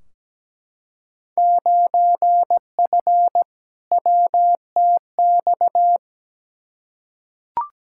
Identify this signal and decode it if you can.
Morse code, heard as a keyed tone, about 17 wpm, 700 Hz. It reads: 9F WTX
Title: Field Day Exchanges Encore - 17wpm